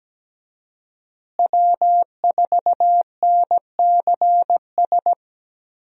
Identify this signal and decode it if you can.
Morse code, heard as a keyed tone, about 17 wpm, 700 Hz. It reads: W4NCS
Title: USA Call Signs - 5 Characters - Repeated 3x - 17wpm